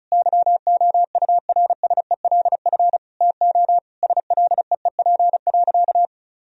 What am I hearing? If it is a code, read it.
Morse code, heard as a keyed tone, about 34 wpm, 700 Hz. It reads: YOURSELF TO SLEEP.
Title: Sentences from Top 1000 Words - Review Part 2 - 35wpm